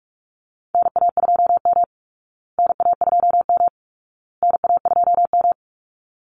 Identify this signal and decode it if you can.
Morse code, heard as a keyed tone, about 44 wpm, 700 Hz. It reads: NA2M NA2M NA2M
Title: Call Signs Repeated 2x - 45wpm